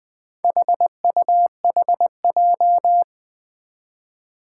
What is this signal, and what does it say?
Morse code, heard as a keyed tone, about 20 wpm, 700 Hz. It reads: HUHJ
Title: Four Challenging Letter-Number Combinations 20wpm